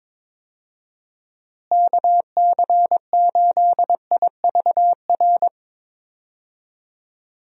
Morse code, heard as a keyed tone, about 22 wpm, 700 Hz. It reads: KC8IVR
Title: USA Call Signs - 6 Characters - Repeated 2x - 22wpm